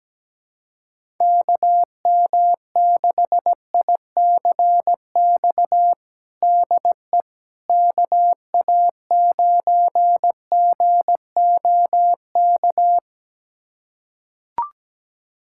Morse code, heard as a keyed tone, about 17 wpm, 700 Hz. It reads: KM6ICX DE KA9GOK
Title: USA Callsign DE Callsign - 6 Characters - 17wpm